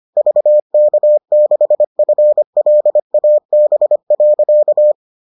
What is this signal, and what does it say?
Morse code, heard as a keyed tone, about 25 wpm, 600 Hz. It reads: VK6FLAB.